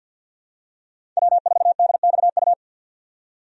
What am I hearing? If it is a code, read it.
Morse code, heard as a keyed tone, about 52 wpm, 700 Hz. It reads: W4DXU
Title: USA Call Signs - 5 Characters - 50wpm